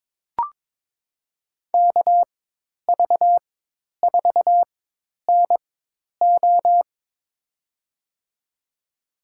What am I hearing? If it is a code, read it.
Morse code, heard as a keyed tone, about 22 wpm, spaced as for 10 wpm, 700 Hz. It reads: KV4NO
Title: USA Call Signs - 5 Characters - 4x Character Spacing - 22wpm